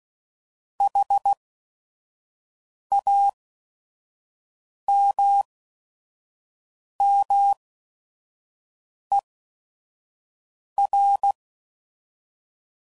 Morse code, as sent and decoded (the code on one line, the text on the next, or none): .... .- -- -- . .-.
HAMMER